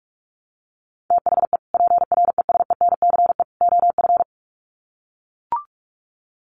Morse code, heard as a keyed tone, about 44 wpm, 700 Hz. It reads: THE PRESENCE OF
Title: Sets of 3 Words 45wpm